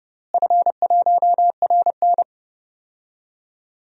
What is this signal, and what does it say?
Morse code, heard as a keyed tone, about 30 wpm, 700 Hz. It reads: F1RN
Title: DXCC-100 Call Signs - 4 Characters - Repeated 2x - 30wpm